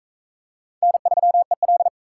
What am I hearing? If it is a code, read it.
Morse code, heard as a keyed tone, about 40 wpm, 700 Hz. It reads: N3EL